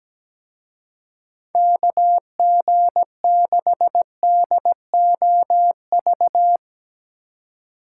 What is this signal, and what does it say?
Morse code, heard as a keyed tone, about 18 wpm, 700 Hz. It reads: KG6DOV
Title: Speed Racing - USA 2x3 Call Signs 17wpm